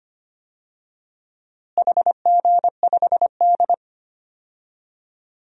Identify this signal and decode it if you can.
Morse code, heard as a keyed tone, about 25 wpm, 700 Hz. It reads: HG5D
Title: DXCC-100 Call Signs - 4 Characters - Repeated 3x - 25wpm